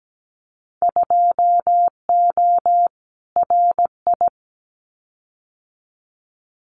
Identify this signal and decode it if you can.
Morse code, heard as a keyed tone, about 17 wpm, 700 Hz. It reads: 2O RI
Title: Winter Field Day Exchanges 17wpm